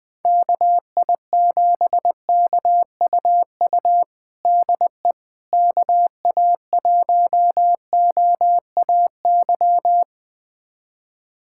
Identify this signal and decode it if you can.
Morse code, heard as a keyed tone, about 20 wpm, 700 Hz. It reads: KI7KUU DE KA1OAY
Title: USA Callsign DE Callsign 20wpm